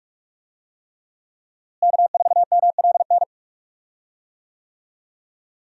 Morse code, heard as a keyed tone, about 42 wpm, 700 Hz. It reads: K4MLN
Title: USA Call Signs - 5 Characters - Repeated 3x - 45wpm